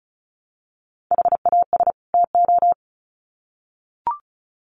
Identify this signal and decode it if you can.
Morse code, heard as a keyed tone, about 34 wpm, 700 Hz. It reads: HAS TO
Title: Sets of 2 Words 35wpm